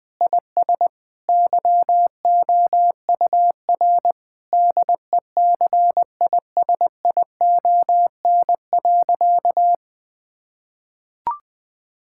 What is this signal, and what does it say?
Morse code, heard as a keyed tone, about 20 wpm, 700 Hz. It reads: IS YOUR DECISION.
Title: Sentences with 8-Word Phrases - 20wpm